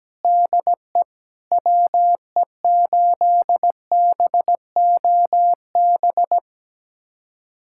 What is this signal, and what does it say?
Morse code, heard as a keyed tone, about 17 wpm, 700 Hz. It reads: DE WE8BOB
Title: USA Callsign DE Callsign 17wpm